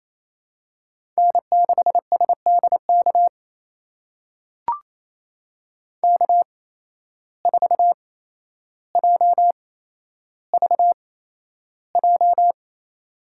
Morse code, heard as a keyed tone, about 27 wpm, spaced as for 12 wpm, 700 Hz. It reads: N6SDK K4JVJ
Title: USA Call Signs - 5 Characters - 8x Character Spacing - 28wpm